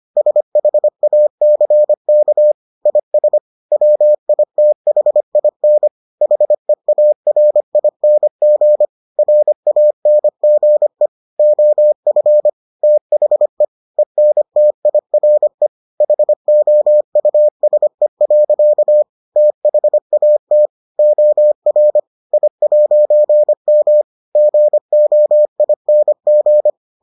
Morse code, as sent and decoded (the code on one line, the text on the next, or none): ... .... .- -.-. -.- / .. ... / .-- .. - .... .. -. / .... .- .-. .. -. --. / .-. .- -. --. / --- ..-. / - .... / . -. - .. .-. / .... --- ..- ... .-.-.- / - .... .- - / --- .-. / .. .----. -- / --. --- .. -. --.
SHACK IS WITHIN HEARING RANGE OF THE ENTIRE HOUSE. THAT OR I'M GOING